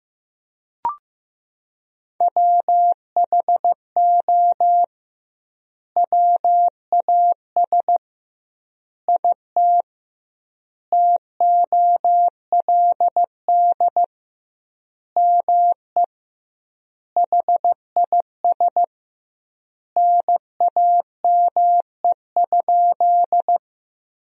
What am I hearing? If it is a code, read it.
Morse code, heard as a keyed tone, about 15 wpm, 700 Hz. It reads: WHO WAS IT TOLD ME HIS NAME?
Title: 2x Word-Spacing - Sentences from Top 400 Words - 15wpm